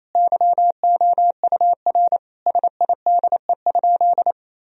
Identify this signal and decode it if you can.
Morse code, heard as a keyed tone, about 27 wpm, 700 Hz. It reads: YOUR SIDE?
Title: Sentences from Top 1000 Words - Review Part 1 - 28wpm